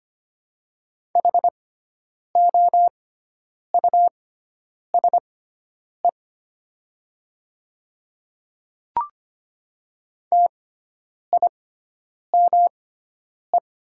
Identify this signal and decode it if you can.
Morse code, heard as a keyed tone, about 25 wpm, spaced as for 9 wpm, 700 Hz. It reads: HOUSE TIME
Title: 6x Character Spacing - Top 200 Words - 25wpm